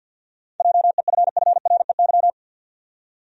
Morse code, heard as a keyed tone, about 51 wpm, 700 Hz. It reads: JEFFREY
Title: USA Common Names - 50wpm